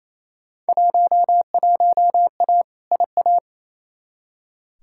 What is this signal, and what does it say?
Morse code, heard as a keyed tone, about 28 wpm, 700 Hz. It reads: 11A IA
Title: Field Day Exchanges Encore - 28wpm